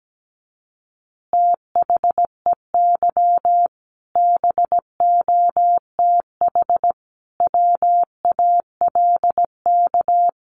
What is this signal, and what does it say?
Morse code, heard as a keyed tone, about 17 wpm, 700 Hz. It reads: THEY BOTH WALK
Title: Additional Easy Phrases 17wpm